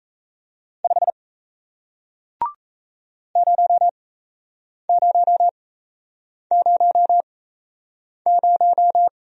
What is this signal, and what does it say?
Morse code, heard as a keyed tone, about 38 wpm, spaced as for 8 wpm, 700 Hz. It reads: 5 0000